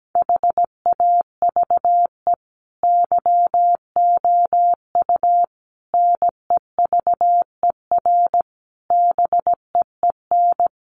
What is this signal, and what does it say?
Morse code, heard as a keyed tone, about 17 wpm, 700 Hz. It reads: HAVE YOU NEVER BEEN
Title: Sentences from Top 300 Words 17wpm